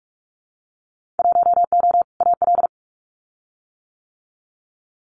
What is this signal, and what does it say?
Morse code, heard as a keyed tone, about 46 wpm, 700 Hz. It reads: 1O AL
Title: Winter Field Day Exchanges 45wpm